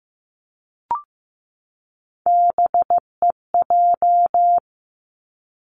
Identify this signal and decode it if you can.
Morse code, heard as a keyed tone, about 15 wpm, 700 Hz. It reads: BEJ
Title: Three Challenging Letter-Number Combinations 15wpm